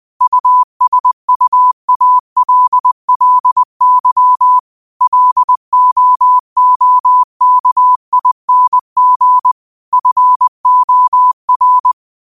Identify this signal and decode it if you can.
Morse code, heard as a keyed tone, about 20 wpm, 1000 Hz. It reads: USUALLY LOOKING FOR